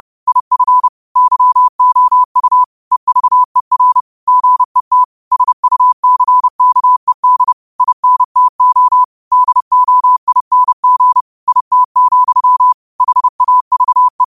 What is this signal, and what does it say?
Morse code, heard as a keyed tone, about 30 wpm, 1000 Hz. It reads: IF YOU EVER GET SUCKED INTO DOING IT, HAVE
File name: qotdfiles / 30wpm